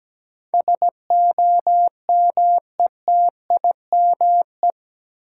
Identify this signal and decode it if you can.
Morse code, heard as a keyed tone, about 17 wpm, 700 Hz. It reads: SOMETIME
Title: Sentences from Top 1500 Words - New Word Drill Alt - 17wpm